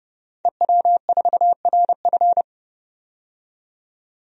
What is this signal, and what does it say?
Morse code, heard as a keyed tone, about 30 wpm, 700 Hz. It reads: EW4RF
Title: DXCC-100 Call Signs - 5 Characters - 30wpm